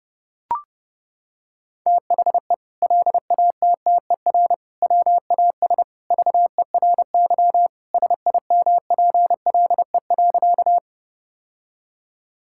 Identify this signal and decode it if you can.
Morse code, heard as a keyed tone, about 30 wpm, 700 Hz. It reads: THE LATTER WAS VERY SIMPLE.
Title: Sentences from Top 1600 Words 30wpm